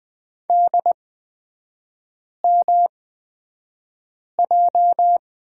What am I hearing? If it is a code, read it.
Morse code, heard as a keyed tone, about 20 wpm, spaced as for 5 wpm, 700 Hz. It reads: DMJ
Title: 229 Lesson 37e - Letter x - Three Characters 20wpm